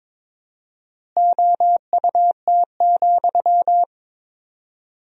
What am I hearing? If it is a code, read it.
Morse code, heard as a keyed tone, about 22 wpm, 700 Hz. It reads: OUT,
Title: 4x Word-Spacing - Sentences from Top 500 Words - 22wpm